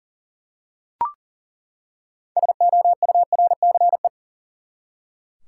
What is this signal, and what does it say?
Morse code, heard as a keyed tone, about 40 wpm, 700 Hz. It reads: SOURCE
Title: Six Letter Words - 40wpm